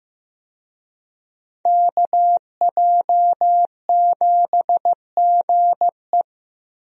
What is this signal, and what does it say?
Morse code, heard as a keyed tone, about 16 wpm, 700 Hz. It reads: KJ7GE
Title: USA Call Signs - 5 Characters - Repeated 3x - 15wpm